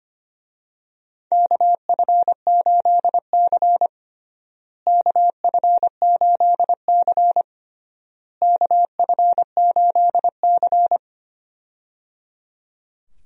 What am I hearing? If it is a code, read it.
Morse code, heard as a keyed tone, about 25 wpm, 700 Hz. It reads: KF8C KF8C KF8C
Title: USA Call Signs - 4 Characters - Repeated 3x - 25wpm